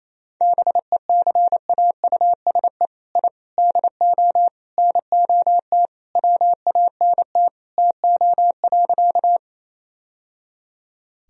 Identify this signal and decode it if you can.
Morse code, heard as a keyed tone, about 28 wpm, 700 Hz. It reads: BECAUSE I DO NOT WANT TO.
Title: Sentences from Top 1000 Words - Review Part 3 - 28wpm